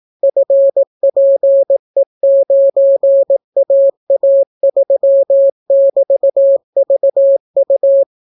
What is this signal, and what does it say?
Morse code, heard as a keyed tone, about 18 wpm, 550 Hz. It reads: FPE9AA3<BT>VU